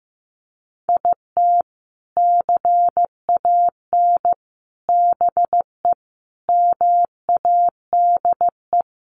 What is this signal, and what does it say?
Morse code, heard as a keyed tone, about 15 wpm, 700 Hz. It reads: IT CAN BE MADE
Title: Sentences from Top 1200 Words 15wpm